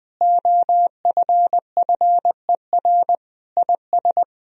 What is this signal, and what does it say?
Morse code, heard as a keyed tone, about 20 wpm, 700 Hz. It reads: OFFER IS
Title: Sentences from Top 1000 Words - Review Part 4 - 20wpm